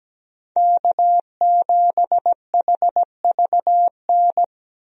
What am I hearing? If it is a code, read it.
Morse code, heard as a keyed tone, about 17 wpm, 700 Hz. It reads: K7HVN
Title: USA Call Signs - 5 Characters - 17wpm